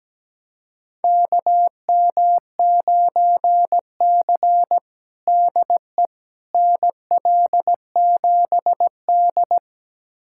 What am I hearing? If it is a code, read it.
Morse code, heard as a keyed tone, about 17 wpm, 700 Hz. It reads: KM9C DE NL7D
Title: USA Callsign DE Callsign - 4 Characters - 17wpm